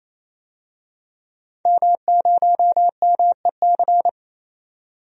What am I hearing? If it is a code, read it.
Morse code, heard as a keyed tone, about 28 wpm, 700 Hz. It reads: M0MEC
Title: DXCC-100 Call Signs - 5 Characters - 28wpm